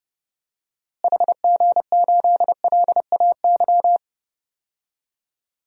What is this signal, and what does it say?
Morse code, heard as a keyed tone, about 31 wpm, 700 Hz. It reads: HG8LAY
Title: DXCC-100 Call Signs - 6 Characters - Repeated 2x - 30wpm